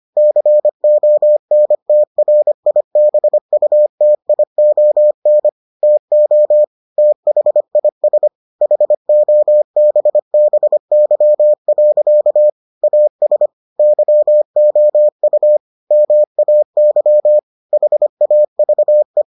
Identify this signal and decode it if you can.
Morse code, heard as a keyed tone, about 25 wpm, 600 Hz. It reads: CONTRIBUTION TO THIS HOBBY. AS YOU MAY HAVE